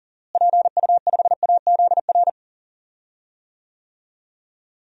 Morse code, heard as a keyed tone, about 40 wpm, 700 Hz. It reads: PU5AZR
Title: DXCC-100 Call Signs - 6 Characters - Repeated 3x - 40wpm